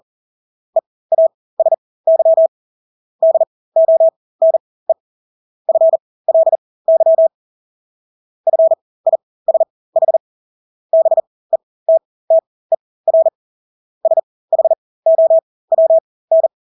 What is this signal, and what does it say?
Morse code, heard as a keyed tone, about 40 wpm, spaced as for 20 wpm, 650 Hz. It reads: EASY DONE FLY FISH BETTER SHOWN
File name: Engelske ord 40-20 wpm